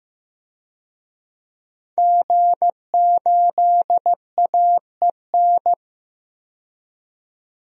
Morse code, heard as a keyed tone, about 15 wpm, 700 Hz. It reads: G8AEN
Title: DXCC-100 Call Signs - 5 Characters - 15wpm